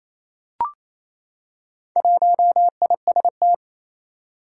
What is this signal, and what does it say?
Morse code, heard as a keyed tone, about 28 wpm, 700 Hz. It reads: 1IST
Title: Four Challenging Letter-Number Combinations 28wpm